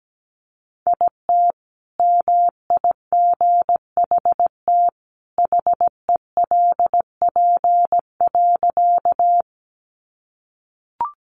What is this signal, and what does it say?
Morse code, heard as a keyed tone, about 17 wpm, 700 Hz. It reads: IT MIGHT HELP.
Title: Sentences from Top 300 Words 17wpm